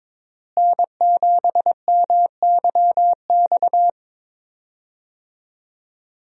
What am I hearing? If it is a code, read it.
Morse code, heard as a keyed tone, about 22 wpm, 700 Hz. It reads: N7MYX
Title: USA Call Signs - 5 Characters - Repeated 2x - 22wpm